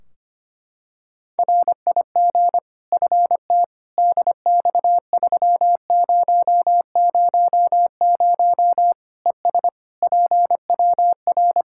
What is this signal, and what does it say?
Morse code, heard as a keyed tone, about 25 wpm, 700 Hz. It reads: RIG FT DX3000 ES PWR
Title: Practice QSO - Exchange 2 - Info - 25wpm